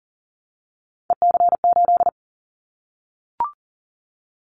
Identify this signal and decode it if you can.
Morse code, heard as a keyed tone, about 40 wpm, 700 Hz. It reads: EC8
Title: Callsign Prefixes for Top 50 DXCC - 40wpm